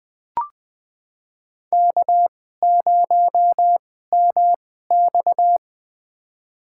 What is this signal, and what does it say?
Morse code, heard as a keyed tone, about 20 wpm, spaced as for 14 wpm, 700 Hz. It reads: K0MX
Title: USA Call Signs - 4 Characters - 2x Character Spacing - 20wpm